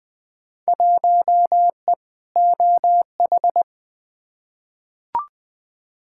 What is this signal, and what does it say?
Morse code, heard as a keyed tone, about 20 wpm, 700 Hz. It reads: 1E OH
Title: Field Day Exchanges - 20wpm